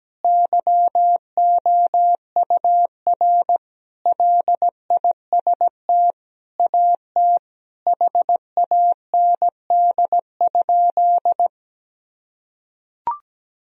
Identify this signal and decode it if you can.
Morse code, heard as a keyed tone, about 17 wpm, 700 Hz. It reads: YOUR LIST AT HAND?
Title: Sentences from Top 1000 Words - Review Part 2 - 17wpm